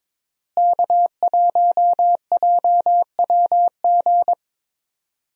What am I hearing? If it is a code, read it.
Morse code, heard as a keyed tone, about 22 wpm, 700 Hz. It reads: K1JWG
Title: USA Call Signs - 5 Characters - Repeated 3x - 22wpm